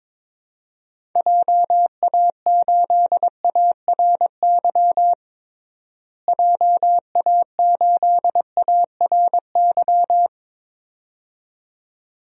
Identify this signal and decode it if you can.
Morse code, heard as a keyed tone, about 22 wpm, 700 Hz. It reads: JA8ARY JA8ARY
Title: DXCC-100 Call Signs - 6 Characters - Repeated 2x - 22wpm